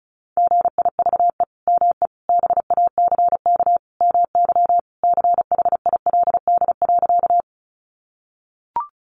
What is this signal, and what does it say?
Morse code, heard as a keyed tone, about 34 wpm, 700 Hz. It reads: GIVE ME BACK MY CHILD.
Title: Sentences from Top 800 Words 35wpm